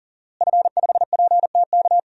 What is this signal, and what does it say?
Morse code, heard as a keyed tone, about 40 wpm, 700 Hz. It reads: F5PTK
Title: DXCC-100 Call Signs - 5 Characters - Repeated 2x - 40wpm